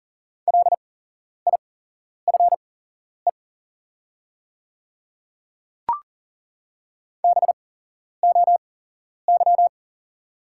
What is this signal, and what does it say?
Morse code, heard as a keyed tone, about 40 wpm, spaced as for 11 wpm, 690 Hz. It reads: LIFE BOY
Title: 8x Character Spacing - Top 500 Words - 40wpm